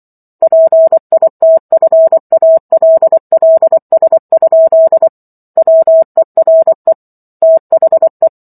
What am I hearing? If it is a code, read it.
Morse code, heard as a keyed tone, about 24 wpm, 650 Hz. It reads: PITFALLS? WERE THE
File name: Chapter 15 - 24 WPM